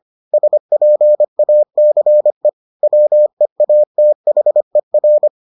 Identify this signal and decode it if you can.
Morse code, heard as a keyed tone, about 25 wpm, 600 Hz. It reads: SPACE WEATHER